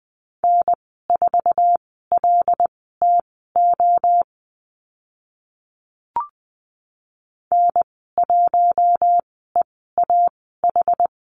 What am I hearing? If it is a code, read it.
Morse code, heard as a keyed tone, about 20 wpm, spaced as for 14 wpm, 700 Hz. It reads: N4LTO N1EAH
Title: 082 Lesson 14g - Number 2 - Call Signs 20wpm